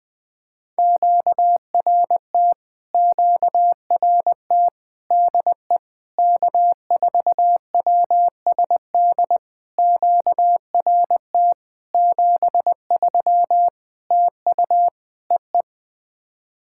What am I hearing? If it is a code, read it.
Morse code, heard as a keyed tone, about 20 wpm, 700 Hz. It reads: QRT QRT DE K4WSD QRT 73 TU EE
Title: POTA QSO Practice - 20wpm